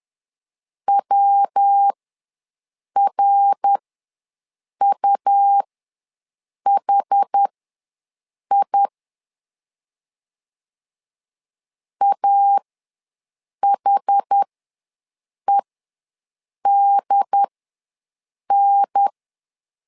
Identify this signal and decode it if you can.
Morse code, heard as a keyed tone, about 11 wpm, spaced as for 6 wpm, 800 Hz. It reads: WRUHI AHEDN
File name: LEZIONI / 3 MWRO / 3 MWR ws 10wpm 2151